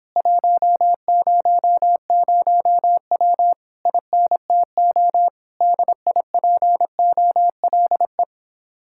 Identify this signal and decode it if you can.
Morse code, heard as a keyed tone, about 26 wpm, 700 Hz. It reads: 100W INTO DIPOLE